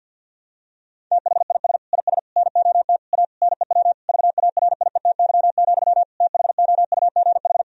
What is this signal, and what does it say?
Morse code, heard as a keyed tone, about 50 wpm, 700 Hz. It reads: THIS IS NOT A NEW VARIETY, THOUGH
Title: Sentences from Top 1000 Words - Review Part 4 - 50wpm